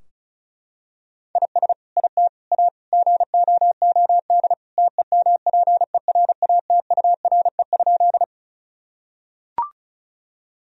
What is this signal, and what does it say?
Morse code, heard as a keyed tone, about 36 wpm, 700 Hz. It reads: IS IT A GOOD TEMPERATURE?
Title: Sentences from Top 1000 Words - Review Part 4 - 35wpm